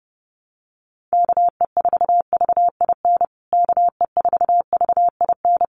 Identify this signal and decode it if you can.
Morse code, heard as a keyed tone, about 30 wpm, 700 Hz. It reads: KE4VIN KE4VIN
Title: USA Call Signs Repeated 3x 30wpm